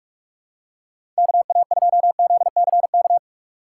Morse code, heard as a keyed tone, about 46 wpm, 700 Hz. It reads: K A 2 Z C K
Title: USA Call Signs - 6 Characters - 6x Character Spacing - 45wpm